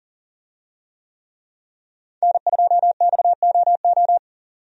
Morse code, heard as a keyed tone, about 40 wpm, 700 Hz. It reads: N2XOO
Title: USA Call Signs - 5 Characters - 40wpm